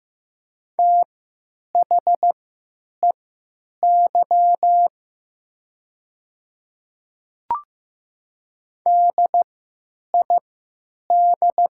Morse code, heard as a keyed tone, about 15 wpm, spaced as for 9 wpm, 700 Hz. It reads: THEY DID
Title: 3x Character Spacing - Top 200 Words - 15wpm